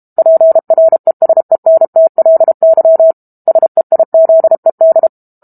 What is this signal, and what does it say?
Morse code, heard as a keyed tone, about 32 wpm, 650 Hz. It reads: PRESENTLY SEIZED
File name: Book Two / 06-Chapter 6 - 32 WPM